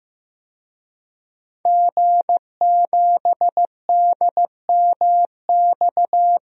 Morse code, heard as a keyed tone, about 16 wpm, 700 Hz. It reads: G7DMX
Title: DXCC-100 Call Signs - 5 Characters - 15wpm